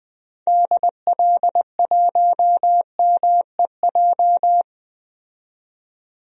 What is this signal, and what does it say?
Morse code, heard as a keyed tone, about 21 wpm, 700 Hz. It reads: DL1MEJ
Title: DXCC-100 Call Signs - 6 Characters - 20wpm